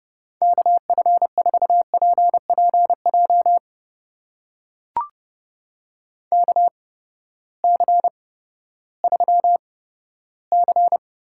Morse code, heard as a keyed tone, about 30 wpm, spaced as for 15 wpm, 700 Hz. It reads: KF4PPJ KC3C
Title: USA Call Signs - 6 Characters - 8x Character Spacing - 30wpm